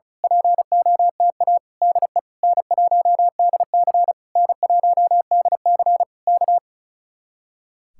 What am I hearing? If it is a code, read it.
Morse code, heard as a keyed tone, about 33 wpm, 700 Hz. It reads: POTA DE N1DC N1DC K
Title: Speed-Racing Common QSO Phrases Encore - 35wpm